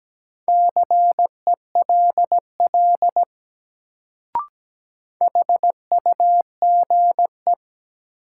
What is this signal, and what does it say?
Morse code, heard as a keyed tone, about 17 wpm, 700 Hz. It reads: CELL HUGE
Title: Sentences from Top 1100 Words - New Word Drill - 17wpm